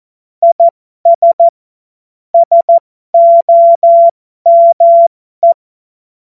Morse code, heard as a keyed tone, about 14 wpm, spaced as for 12 wpm, 680 Hz. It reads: IS SOME
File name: Morsekurs / Witze